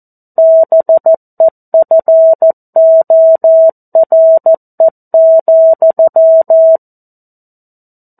Morse code, heard as a keyed tone, about 14 wpm, 650 Hz. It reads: BEFORE,
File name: 02-Chapter-02 14 WPM